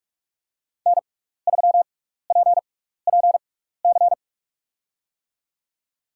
Morse code, heard as a keyed tone, about 44 wpm, 700 Hz. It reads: N 3 P P C
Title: USA Call Signs - 5 Characters - 6x Character Spacing - 45wpm